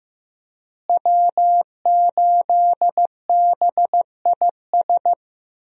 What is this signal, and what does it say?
Morse code, heard as a keyed tone, about 15 wpm, 700 Hz. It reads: W8BIS
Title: USA Call Signs - 5 Characters - Repeated 3x - 15wpm